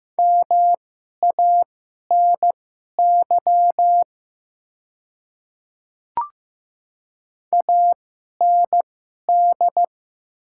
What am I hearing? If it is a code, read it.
Morse code, heard as a keyed tone, about 15 wpm, spaced as for 11 wpm, 700 Hz. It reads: MANY AND
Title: 2x Character Spacing - Top 200 Words - 15wpm